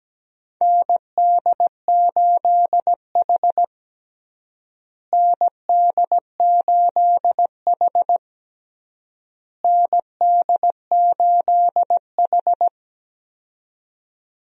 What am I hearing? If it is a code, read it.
Morse code, heard as a keyed tone, about 17 wpm, 700 Hz. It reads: ND8H ND8H ND8H
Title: USA Call Signs - 4 Characters - Repeated 3x - 17wpm